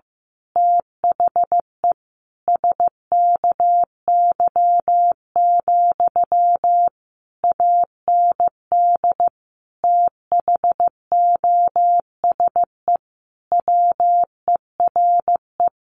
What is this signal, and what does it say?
Morse code, heard as a keyed tone, about 15 wpm, 700 Hz. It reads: THE SKY, AND THOSE WERE
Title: A Sound Of Thunder 15wpm